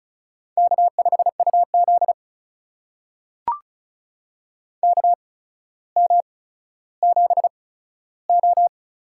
Morse code, heard as a keyed tone, about 36 wpm, spaced as for 15 wpm, 700 Hz. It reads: K5UZ KM7O
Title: USA Call Signs - 4 Characters - 8x Character Spacing - 35wpm